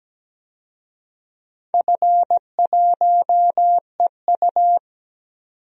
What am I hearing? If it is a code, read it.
Morse code, heard as a keyed tone, about 17 wpm, 700 Hz. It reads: F1EU
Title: DXCC-100 Call Signs - 4 Characters - 17wpm